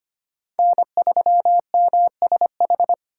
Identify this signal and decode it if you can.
Morse code, heard as a keyed tone, about 25 wpm, 700 Hz. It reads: N3MSH
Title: USA Call Signs - 3x Character Spacing - 25wpm